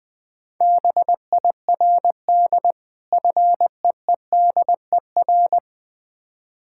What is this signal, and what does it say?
Morse code, heard as a keyed tone, about 20 wpm, 690 Hz. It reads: BIRD FEEDER
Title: Difficult Words with Hints - 20wpm